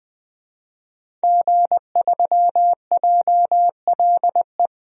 Morse code, heard as a keyed tone, about 20 wpm, 700 Hz. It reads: G3JLE
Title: DXCC-100 Call Signs - 5 Characters - 20wpm